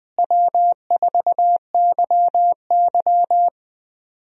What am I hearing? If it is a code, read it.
Morse code, heard as a keyed tone, about 20 wpm, 700 Hz. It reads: W4YY
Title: USA Call Signs - 4 Characters - Repeated 3x - 20wpm